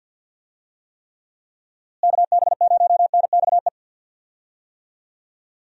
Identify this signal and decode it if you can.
Morse code, heard as a keyed tone, about 49 wpm, 700 Hz. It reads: KB0NXE
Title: USA Call Signs - 6 Characters - Repeated 3x - 50wpm